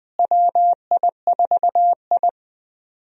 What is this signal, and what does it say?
Morse code, heard as a keyed tone, about 20 wpm, 700 Hz. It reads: WI4I
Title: USA Call Signs - 4 Characters - Repeated 3x - 20wpm